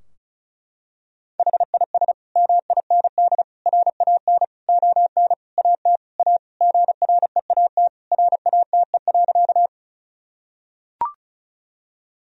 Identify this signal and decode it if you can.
Morse code, heard as a keyed tone, about 36 wpm, 700 Hz. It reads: HIS MIND RAN ON AT A GREAT RATE.
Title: Sentences from Top 1000 Words - Review Part 4 - 35wpm